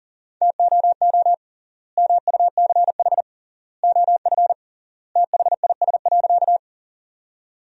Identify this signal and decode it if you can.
Morse code, heard as a keyed tone, about 40 wpm, 700 Hz. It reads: TOO MUCH OF THIS.
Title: 3x Word-Spacing - Sentences from Top 200 Words - 40wpm